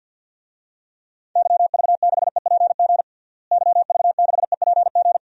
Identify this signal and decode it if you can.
Morse code, heard as a keyed tone, about 50 wpm, 700 Hz. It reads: YV6EPG YV6EPG
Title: DXCC-100 Call Signs - 6 Characters - Repeated 2x - 50wpm